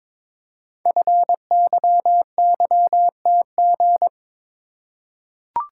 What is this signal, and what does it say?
Morse code, heard as a keyed tone, about 22 wpm, 700 Hz. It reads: FYYTG
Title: Code Groups - 5 Letters - 22wpm